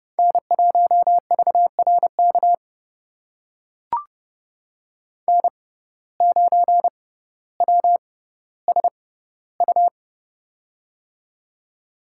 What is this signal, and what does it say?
Morse code, heard as a keyed tone, about 30 wpm, spaced as for 15 wpm, 700 Hz. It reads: N1VRK N9WSU
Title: USA Call Signs - 6x Character Spacing - 30wpm